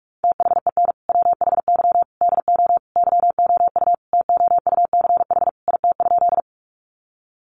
Morse code, heard as a keyed tone, about 44 wpm, 700 Hz. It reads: THEN WHY DO YOU TOUCH IT?
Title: Sentences from Top 1200 Words 45wpm